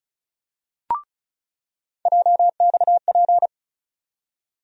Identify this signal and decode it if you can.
Morse code, heard as a keyed tone, about 36 wpm, 700 Hz. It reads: JXP
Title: USA Call Sign Postfixes - 3 Letters 35wpm